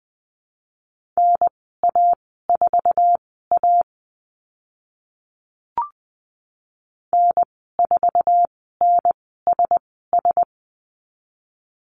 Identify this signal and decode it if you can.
Morse code, heard as a keyed tone, about 20 wpm, spaced as for 14 wpm, 700 Hz. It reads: NA4A N4NSS
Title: 048 Lesson 9g - Number 4 - Call Signs 20wpm